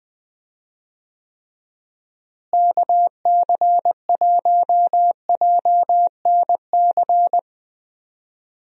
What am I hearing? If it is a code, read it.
Morse code, heard as a keyed tone, about 20 wpm, 700 Hz. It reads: KC1JNC
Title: USA Call Signs - 6 Characters - Repeated 2x - 20wpm